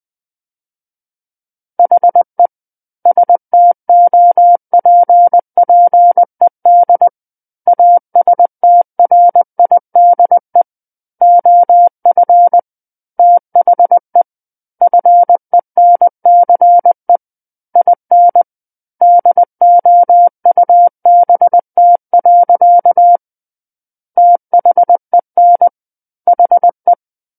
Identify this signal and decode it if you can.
Morse code, heard as a keyed tone, about 20 wpm, 700 Hz. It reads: HE STOPPED ASTRIDE OF THE FENCE IN DOUBT. THEN HE
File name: 20 WPM